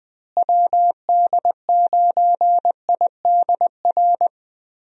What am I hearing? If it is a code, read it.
Morse code, heard as a keyed tone, about 20 wpm, 700 Hz. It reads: WD9IDR
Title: USA Call Signs - 6 Characters - Repeated 2x - 20wpm